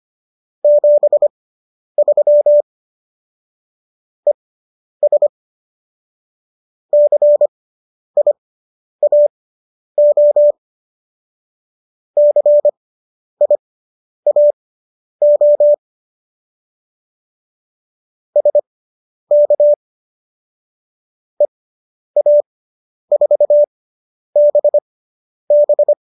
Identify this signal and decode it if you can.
Morse code, heard as a keyed tone, about 25 wpm, spaced as for 10 wpm, 590 Hz. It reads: 73 ES CIAO CIAO SK EA4BB